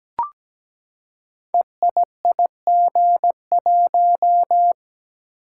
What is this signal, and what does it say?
Morse code, heard as a keyed tone, about 17 wpm, 700 Hz. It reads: EIIG1
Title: Code Groups - 5 Letter:Numbers - 17wpm